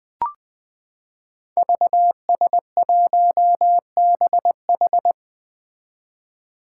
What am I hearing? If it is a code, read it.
Morse code, heard as a keyed tone, about 20 wpm, 700 Hz. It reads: VS1BH
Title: Five Challenging Letter-Number Combinations 20wpm